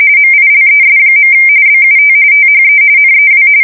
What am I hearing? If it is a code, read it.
RTTY, about 45.45 baud. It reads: MILES. BUT THE NEW
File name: fsk170 / USS-New-Jersey